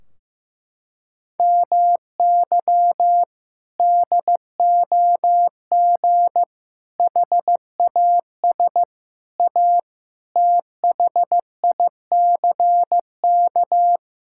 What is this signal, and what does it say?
Morse code, heard as a keyed tone, about 15 wpm, 700 Hz. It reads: MY DOG HAS A THICK
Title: Sentences from Top 1600 Words 15wpm